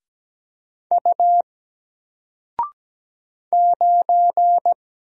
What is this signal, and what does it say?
Morse code, heard as a keyed tone, about 17 wpm, 700 Hz. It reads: U 9
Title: Single Letter-Number Alt 17wpm